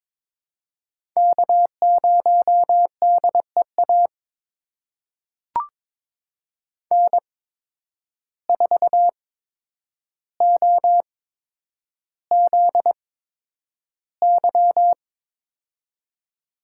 Morse code, heard as a keyed tone, about 22 wpm, spaced as for 9 wpm, 700 Hz. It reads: K0DEA N4OZY
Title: USA Call Signs - 5 Characters - 8x Character Spacing - 22wpm